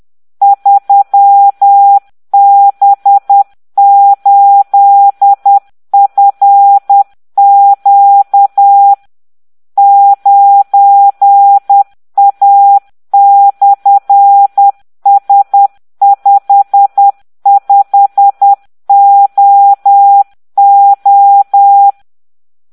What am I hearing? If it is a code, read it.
Morse code, heard as a keyed tone, about 10 wpm, 800 Hz. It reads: 3B8FQ 9A/S55OO